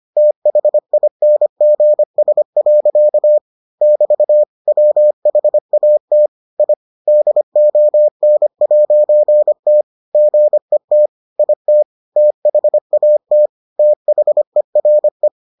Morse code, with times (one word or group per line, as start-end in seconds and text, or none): 0.2-3.4 THINGS.
3.8-4.4 <BT>
4.7-6.3 WHAT
6.6-6.7 I
7.1-9.8 DON'T
10.1-11.1 GET
11.4-11.8 IT
12.2-13.5 THAT
13.8-15.3 THERE